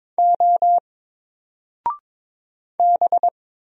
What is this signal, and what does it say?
Morse code, heard as a keyed tone, about 23 wpm, 700 Hz. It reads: O B
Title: Single Character (Letter:Number:Punctuation:Prosign) Alt 22wpm